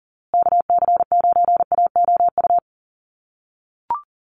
Morse code, heard as a keyed tone, about 40 wpm, 700 Hz. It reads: KC9AOU
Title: USA Call Signs Repeated 3x 40wpm